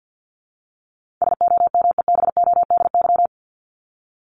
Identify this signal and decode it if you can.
Morse code, heard as a keyed tone, about 50 wpm, 700 Hz. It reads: SOMEBODY
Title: Top 1751-2000 Words 50wpm